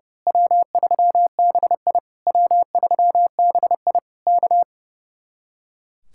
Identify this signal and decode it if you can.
Morse code, heard as a keyed tone, about 28 wpm, 700 Hz. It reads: W3BI W3BI K
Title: Speed-Racing Common QSO Phrases - 30wpm